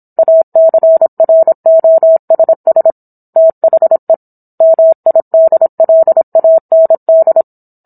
Morse code, heard as a keyed tone, about 27 wpm, 650 Hz. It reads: ACROSS THE MIDLAND
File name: Book One / 17-Chapter 17 - 26 WPM